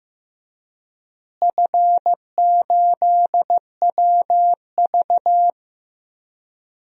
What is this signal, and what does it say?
Morse code, heard as a keyed tone, about 16 wpm, 700 Hz. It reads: F8WV
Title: DXCC-100 Call Signs - 4 Characters - Repeated 3x - 15wpm